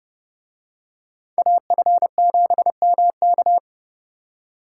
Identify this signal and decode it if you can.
Morse code, heard as a keyed tone, about 30 wpm, 700 Hz. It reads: AF7MK